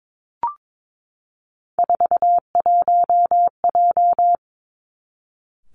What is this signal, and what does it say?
Morse code, heard as a keyed tone, about 22 wpm, 700 Hz. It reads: 41J
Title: Three Challenging Letter-Number Combinations 22wpm